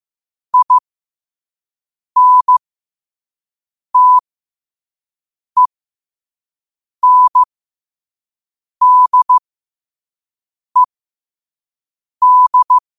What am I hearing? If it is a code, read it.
Morse code, heard as a keyed tone, about 15 wpm, spaced as for 5 wpm, 1000 Hz. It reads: INTENDED